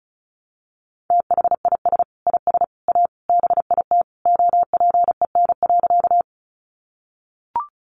Morse code, heard as a keyed tone, about 34 wpm, 700 Hz. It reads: THIS IS A BIT OPEN.
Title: Sentences from Top 900 Words 35wpm